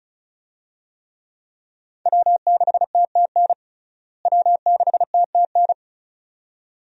Morse code, heard as a keyed tone, about 36 wpm, 700 Hz. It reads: W6TTN W6TTN
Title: USA Call Signs - 5 Characters - Repeated 2x - 35wpm